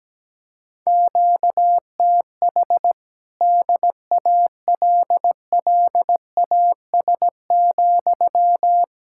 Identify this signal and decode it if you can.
Morse code, heard as a keyed tone, about 17 wpm, 700 Hz. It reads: QTH DALLAS,
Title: Speed-Racing Common QSO Phrases Encore - 17wpm